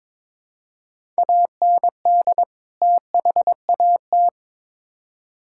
Morse code, heard as a keyed tone, about 22 wpm, 700 Hz. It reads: AND THAT
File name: Build A Phrase - 2-Word Phrases - 22wpm